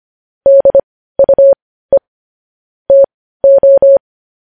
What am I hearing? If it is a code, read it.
Morse code, heard as a keyed tone, about 25 wpm, spaced as for 15 wpm, 550 Hz. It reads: DUE TO
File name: podcast / News headlines-1x-25x15